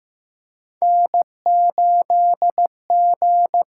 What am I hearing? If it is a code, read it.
Morse code, heard as a keyed tone, about 15 wpm, 700 Hz. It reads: N8G
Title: USA Call Signs - 6x Character Spacing - 15wpm